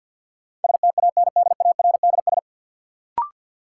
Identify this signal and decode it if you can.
Morse code, heard as a keyed tone, about 52 wpm, 700 Hz. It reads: STANDARDS
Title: Sentences from Top 1000 Words - Review Part 6 - New Word Drill - 50wpm